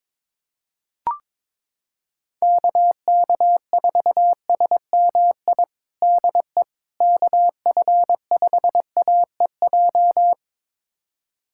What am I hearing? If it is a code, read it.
Morse code, heard as a keyed tone, about 22 wpm, 700 Hz. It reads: KK4SMI DE KF5AEJ
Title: USA Callsign DE Callsign - 6 Characters - 22wpm